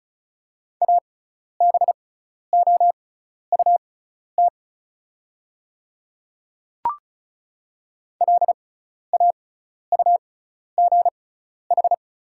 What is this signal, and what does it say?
Morse code, heard as a keyed tone, about 36 wpm, spaced as for 12 wpm, 700 Hz. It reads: ABOUT LAUGH
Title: 6x Character Spacing - Top 500 Words - 35wpm